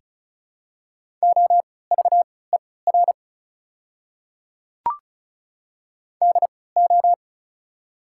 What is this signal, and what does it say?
Morse code, heard as a keyed tone, about 36 wpm, spaced as for 20 wpm, 700 Hz. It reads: OVER DO
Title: 3x Character Spacing - Top 300 Words - 35wpm